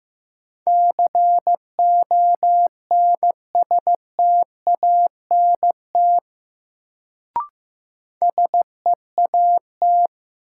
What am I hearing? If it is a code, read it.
Morse code, heard as a keyed tone, about 15 wpm, 700 Hz. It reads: CONSTANT SEAT
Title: Sentences from Top 1500 Words - New Word Drill - 15wpm